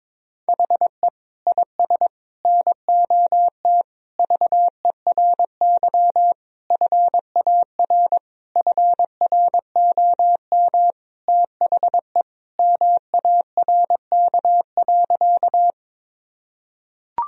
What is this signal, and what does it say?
Morse code, heard as a keyed tone, about 22 wpm, 700 Hz. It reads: HE IS NOT VERY FAR FROM THE MARK.
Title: Sentences from Top 1000 Words - Review Part 1 - 22wpm